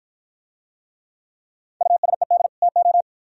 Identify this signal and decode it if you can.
Morse code, heard as a keyed tone, about 53 wpm, 700 Hz. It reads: USED TO